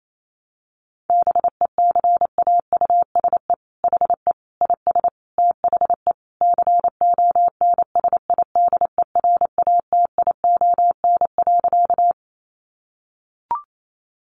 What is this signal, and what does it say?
Morse code, heard as a keyed tone, about 28 wpm, 700 Hz. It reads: BECAUSE HE IS THE CONSIDERATION.
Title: Sentences from Top 1500 Words 28wpm